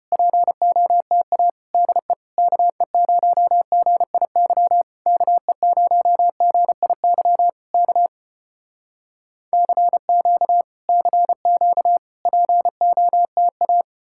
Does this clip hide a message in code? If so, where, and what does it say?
Yes: Morse code from 0.1 to 13.8 s, heard as a keyed tone, about 33 wpm, 700 Hz. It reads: POTA DE KE0GIY KE0GIY K CQ CQ POTA